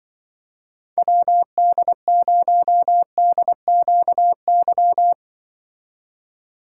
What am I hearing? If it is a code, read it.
Morse code, heard as a keyed tone, about 23 wpm, 700 Hz. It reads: WD0DQY
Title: Speed Racing - USA 2x3 Call Signs 20wpm